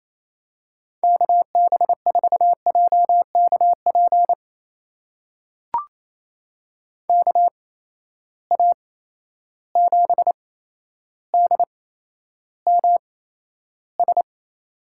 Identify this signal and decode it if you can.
Morse code, heard as a keyed tone, about 28 wpm, spaced as for 12 wpm, 700 Hz. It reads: KB4JKP KA7DMS
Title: USA Call Signs - 6 Characters - 8x Character Spacing - 28wpm